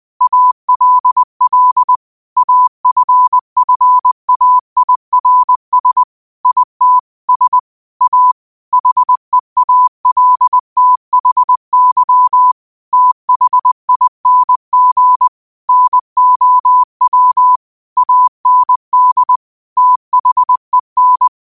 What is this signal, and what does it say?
Morse code, heard as a keyed tone, about 20 wpm, 1000 Hz. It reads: ALL AFFAIRS IT S A HEALTHY THING NOW AND THEN
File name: qotdfiles / 20wpm